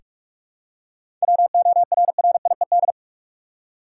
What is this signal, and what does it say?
Morse code, heard as a keyed tone, about 46 wpm, 700 Hz. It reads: WORRIED
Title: Seven Letter Words - 45wpm